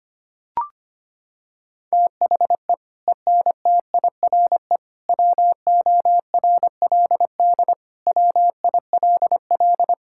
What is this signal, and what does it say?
Morse code, heard as a keyed tone, about 25 wpm, 700 Hz. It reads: THE ENTIRE WORLD WILL
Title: Sentences from Top 1000 Words - Review Part 5 - 25wpm